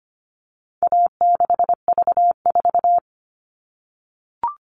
Morse code, heard as a keyed tone, about 25 wpm, 700 Hz. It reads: A6V4
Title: All Cut Numbers - 4-Digit 25wpm